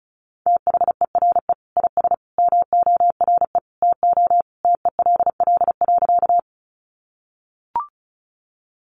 Morse code, heard as a keyed tone, about 34 wpm, 700 Hz. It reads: THERE IS MORE TO TELL.
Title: Sentences from Top 200 Words 35wpm